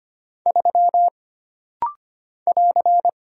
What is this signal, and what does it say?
Morse code, heard as a keyed tone, about 25 wpm, 700 Hz. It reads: 3 <AR>
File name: Single Character (Letter:Number:Punctuation:Prosign) - Rapid-Fire - 25wpm